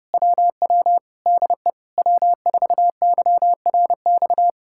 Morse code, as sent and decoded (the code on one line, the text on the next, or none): .-- .-- / -.. / .-- ....- -.-- .-. -..-
WW DE W4YRX